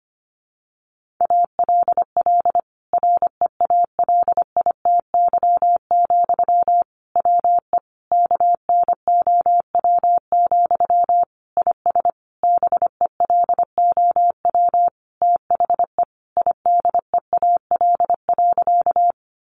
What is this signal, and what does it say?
Morse code, heard as a keyed tone, about 25 wpm, 700 Hz. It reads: ALL REALITY, WE KNOW, IS BELOW THE IDEAL.
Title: Sentences from Top 1400 Words 25wpm